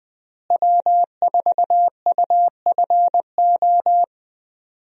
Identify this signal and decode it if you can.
Morse code, heard as a keyed tone, about 21 wpm, 700 Hz. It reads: W4UFO
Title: USA Call Signs - 5 Characters - Repeated 2x - 20wpm